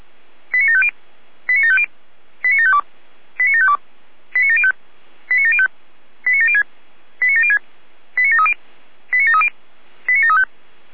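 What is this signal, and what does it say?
Five-tone selective call (ZVEI), heard as quick runs of five tones, tones of about 70 ms. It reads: 89750 89750 89752 89752 89705 89705 89706 89706 89730 89730 89735
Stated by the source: ZVEI five-tone call 89531 received by SDR.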